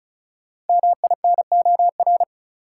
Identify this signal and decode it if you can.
Morse code, heard as a keyed tone, about 34 wpm, 700 Hz. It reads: MINOR
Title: Sentences from Top 1500 Words - New Word Drill Alt - 35wpm